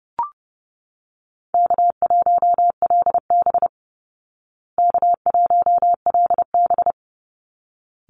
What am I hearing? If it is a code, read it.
Morse code, heard as a keyed tone, about 30 wpm, 700 Hz. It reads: K1LB K1LB
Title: Easy Call Signs Repeated 2x - 30wpm